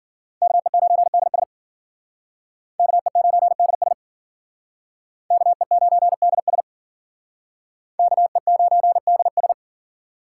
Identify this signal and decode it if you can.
Morse code, heard as a keyed tone, about 51 wpm, 700 Hz. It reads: KE9DS KE9DS KE9DS KE9DS